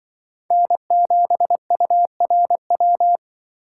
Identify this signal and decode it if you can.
Morse code, heard as a keyed tone, about 25 wpm, 700 Hz. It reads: N7URW
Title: Speed Racing - USA 1x3 Call Signs 20wpm